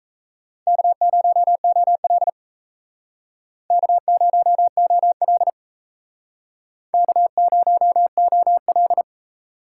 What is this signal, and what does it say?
Morse code, heard as a keyed tone, about 34 wpm, 700 Hz. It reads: K0OL K0OL K0OL
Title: Speed Racing - USA 1x2 Call Signs 28wpm